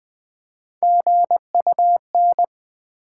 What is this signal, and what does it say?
Morse code, heard as a keyed tone, about 20 wpm, 700 Hz. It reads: GUN
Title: Build A Word - 1 Letter At A Time - 3-Letter Words - 20wpm